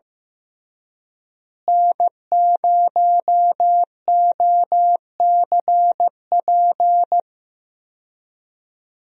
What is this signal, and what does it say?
Morse code, heard as a keyed tone, about 15 wpm, 700 Hz. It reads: N0OCP
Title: USA Call Signs - 5 Characters - Repeated 3x - 15wpm